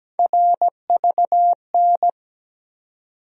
Morse code, heard as a keyed tone, about 17 wpm, 700 Hz. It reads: RVN
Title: USA Call Sign Postfixes - 3 Letters 17wpm